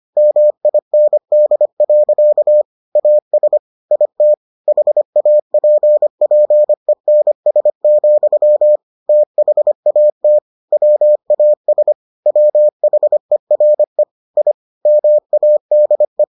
Morse code, heard as a keyed tone, about 25 wpm, 600 Hz. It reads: MIND. AS IT HAPPENS, THAT WAS WHERE I MADE